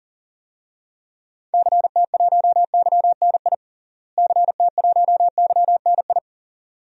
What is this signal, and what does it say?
Morse code, heard as a keyed tone, about 37 wpm, 700 Hz. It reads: CT1YNI CT1YNI
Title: DXCC-100 Call Signs - 6 Characters - Repeated 2x - 40wpm